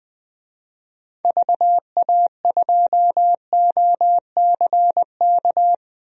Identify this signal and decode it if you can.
Morse code, heard as a keyed tone, about 20 wpm, 700 Hz. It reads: VA2OCK
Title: DXCC-100 Call Signs - 6 Characters - 20wpm